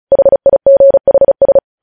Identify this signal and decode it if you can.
Morse code, heard as a keyed tone, about 36 wpm, 550 Hz. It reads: HIGHS
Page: news headlines in morse code at 35 wpm